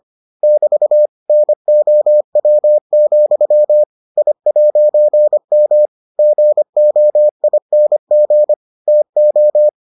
Morse code, heard as a keyed tone, about 25 wpm, 600 Hz. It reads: <BT> NOW, I'M GOING TO